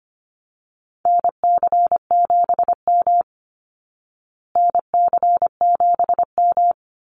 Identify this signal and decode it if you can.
Morse code, heard as a keyed tone, about 25 wpm, 700 Hz. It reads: NC7M NC7M
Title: Call Signs Repeated 3x - 25wpm